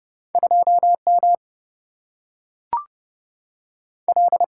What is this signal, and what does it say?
Morse code, heard as a keyed tone, about 30 wpm, 700 Hz. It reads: 2 M L
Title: Single-Letter Single-Number 30wpm